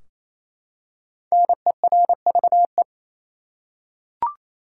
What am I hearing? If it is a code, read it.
Morse code, heard as a keyed tone, about 28 wpm, 700 Hz. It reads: NERVE
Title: Five Letter Words - 28wpm